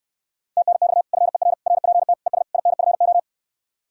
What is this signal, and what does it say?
Morse code, heard as a keyed tone, about 67 wpm, 700 Hz. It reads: TNX FER RPT ES INFO